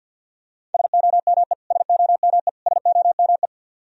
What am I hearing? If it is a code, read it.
Morse code, heard as a keyed tone, about 46 wpm, 700 Hz. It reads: SOME SOME SOME
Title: Top 100 Words Repeated 50wpm